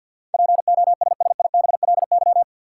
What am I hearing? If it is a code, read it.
Morse code, heard as a keyed tone, about 53 wpm, 700 Hz. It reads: POSSIBLY
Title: Sentences from Top 1300 Words - New Word Drill - 50wpm